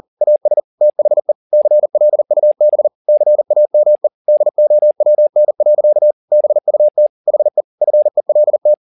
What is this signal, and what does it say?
Morse code, heard as a keyed tone, about 40 wpm, 600 Hz. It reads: AS THE CLUB CAME DOWN. BUT HE FELT
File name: the old man and the sea 2 40wpm